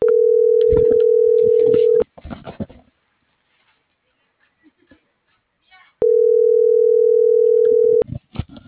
Ringback tone (North America), 440 + 480 Hz.
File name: CENTRAL MULTIMODAL / out rodolfo.fuentes